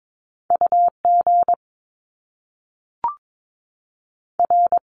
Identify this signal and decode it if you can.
Morse code, heard as a keyed tone, about 22 wpm, 700 Hz. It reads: U G R
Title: Top Two Letter Combinations in Words 22wpm